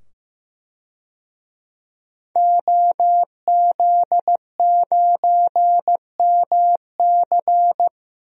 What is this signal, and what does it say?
Morse code, heard as a keyed tone, about 15 wpm, 700 Hz. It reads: OZ9MC
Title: DXCC-100 Call Signs - 5 Characters - 15wpm